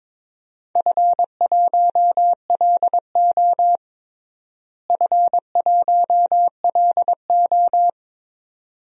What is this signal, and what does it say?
Morse code, heard as a keyed tone, about 22 wpm, 700 Hz. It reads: F1LO F1LO
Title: DXCC-100 Call Signs - 4 Characters - Repeated 2x - 22wpm